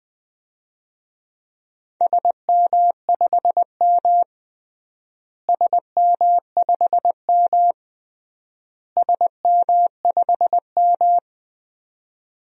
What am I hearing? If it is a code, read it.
Morse code, heard as a keyed tone, about 20 wpm, 690 Hz. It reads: SM5M SM5M SM5M
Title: DXCC-100 Call Signs - 4 Characters - Repeated 3x - 20wpm